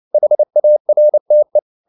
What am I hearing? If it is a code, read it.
Morse code, heard as a keyed tone, about 29 wpm, 600 Hz. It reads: HARTE